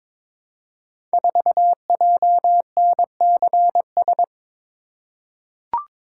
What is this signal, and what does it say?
Morse code, heard as a keyed tone, about 23 wpm, 700 Hz. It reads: 4JNCS
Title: Code Groups - 5 Letter:Numbers - 22wpm